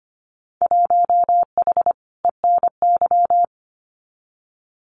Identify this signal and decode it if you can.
Morse code, heard as a keyed tone, about 25 wpm, 700 Hz. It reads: 1H ENY
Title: Winter Field Day Exchanges 25wpm